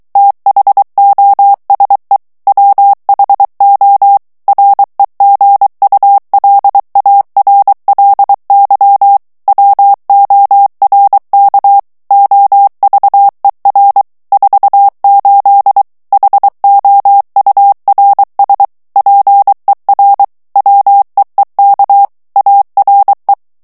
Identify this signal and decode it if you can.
Morse code, heard as a keyed tone, about 23 wpm, 800 Hz. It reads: THOSE WHO REGULARLY WORK OVER 48 HOURS PER WEEK ARE
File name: morse mp3 / ANGLAIS texte clair 1 vitesse 22 mots minute